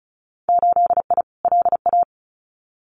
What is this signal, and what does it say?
Morse code, heard as a keyed tone, about 36 wpm, 700 Hz. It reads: 8I LA
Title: Winter Field Day Exchanges Encore 35wpm